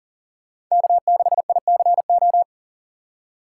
Morse code, heard as a keyed tone, about 40 wpm, 700 Hz. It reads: K6ICO
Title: USA Call Signs - 5 Characters - 4x Character Spacing - 40wpm